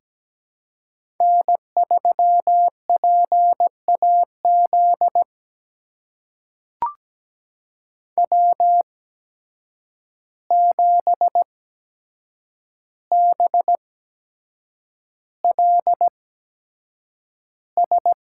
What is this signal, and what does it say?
Morse code, heard as a keyed tone, about 17 wpm, spaced as for 7 wpm, 700 Hz. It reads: N3PAZ W7BLS
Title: USA Call Signs - 5 Characters - 8x Character Spacing - 17wpm